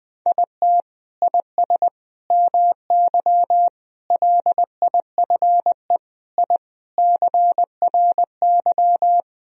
Morse code, heard as a keyed tone, about 20 wpm, 700 Hz. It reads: IT IS MY LIFE I CRY
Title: Sentences from Top 1000 Words - Review Part 2 - 20wpm